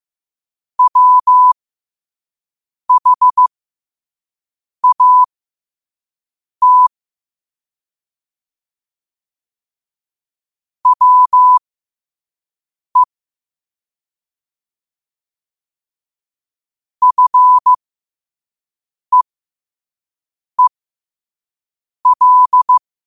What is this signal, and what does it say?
Morse code, heard as a keyed tone, about 15 wpm, spaced as for 5 wpm, 1000 Hz. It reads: WHAT WE FEEL
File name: qotdfiles / 5wpm